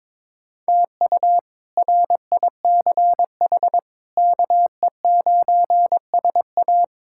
Morse code, heard as a keyed tone, about 22 wpm, 700 Hz. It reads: TU RICH KE9SA
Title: K1USN SST QSO Practice - 22wpm